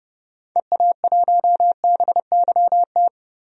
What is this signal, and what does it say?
Morse code, heard as a keyed tone, about 30 wpm, 700 Hz. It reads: EA1BYT
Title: DXCC-100 Call Signs - 6 Characters - Repeated 3x - 30wpm